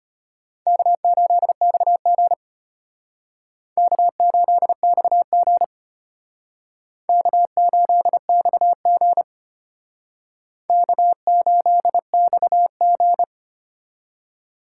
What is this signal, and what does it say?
Morse code, heard as a keyed tone, about 32 wpm, 700 Hz. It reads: K8XG K8XG K8XG K8XG